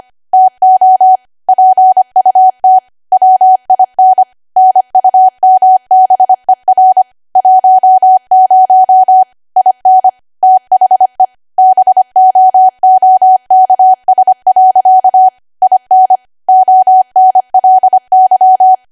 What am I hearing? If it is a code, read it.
Morse code, heard as a keyed tone, about 25 wpm, 750 Hz. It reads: TO PUT WIN NUMBER 10 IN THE BOOKS. IN ONLY
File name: cwa / W1AW-Level3 / 25WPM